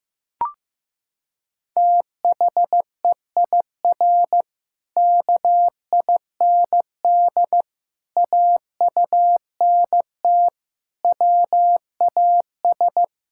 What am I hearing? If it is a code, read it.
Morse code, heard as a keyed tone, about 16 wpm, 700 Hz. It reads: THEIR KIND AUNT WAS
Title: Pangram Challenge v2 - 15wpm